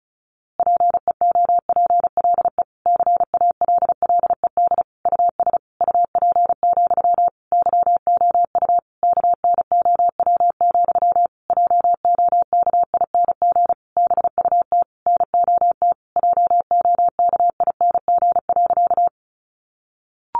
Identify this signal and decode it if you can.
Morse code, heard as a keyed tone, about 34 wpm, 700 Hz. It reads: PEOPLE CALLED US UP, YOU KNOW, JOKING BUT NOT JOKING.
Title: A Sound Of Thunder 35wpm